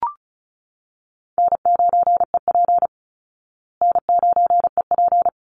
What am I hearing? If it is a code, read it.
Morse code, heard as a keyed tone, about 34 wpm, 700 Hz. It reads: N9EP N9EP
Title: Easy Call Signs Repeated 3x - 35wpm